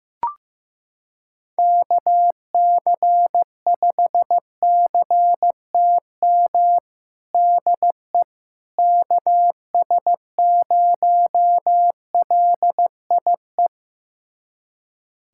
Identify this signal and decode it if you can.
Morse code, heard as a keyed tone, about 16 wpm, 700 Hz. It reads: KC5CTM DE KS0LIE
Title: USA Callsign DE Callsign - 6 Characters - 15wpm